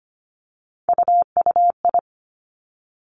Morse code, heard as a keyed tone, about 25 wpm, 700 Hz. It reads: UUI
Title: Three Challenging Letter-Number Combinations 25wpm